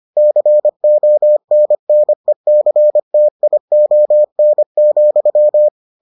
Morse code, heard as a keyed tone, about 25 wpm, 600 Hz. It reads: CONNECTION,